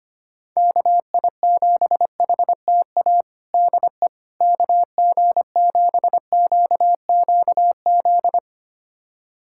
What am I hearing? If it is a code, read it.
Morse code, heard as a keyed tone, about 25 wpm, 700 Hz. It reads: KI7HTA DE KG7QQZ
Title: USA Callsign DE Callsign - 6 Characters - 25wpm